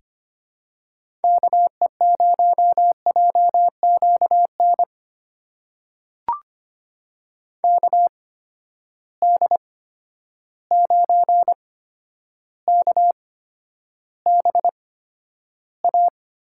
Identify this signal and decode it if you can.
Morse code, heard as a keyed tone, about 25 wpm, spaced as for 11 wpm, 700 Hz. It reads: KE0JQN KD9KBA
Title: USA Call Signs - 6 Characters - 8x Character Spacing - 25wpm